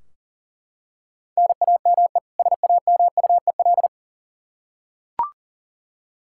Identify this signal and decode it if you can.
Morse code, heard as a keyed tone, about 38 wpm, 700 Hz. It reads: NAME SAMUEL
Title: Practice QSO - Exchange 1 - Beginning - 40wpm